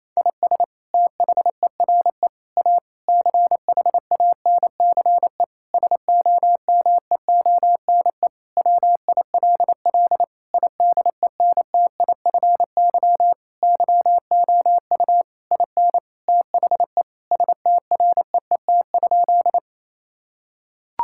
Morse code, heard as a keyed tone, about 28 wpm, 690 Hz. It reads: IS THERE A CHANCE SOMEONE WILL IDENTIFY YOU IN THE STREET?
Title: Sentences from Top 1000 Words - Review Part 6 - 28wpm